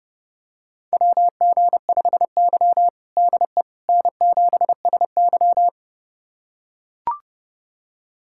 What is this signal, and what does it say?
Morse code, heard as a keyed tone, about 30 wpm, 700 Hz. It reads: WG5Y DE N7SY
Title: USA Callsign DE Callsign - 4 Characters - 30wpm